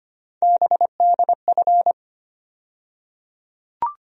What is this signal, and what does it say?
Morse code, heard as a keyed tone, about 25 wpm, 700 Hz. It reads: BDF
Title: USA Call Sign Postfixes - 3 Letters 25wpm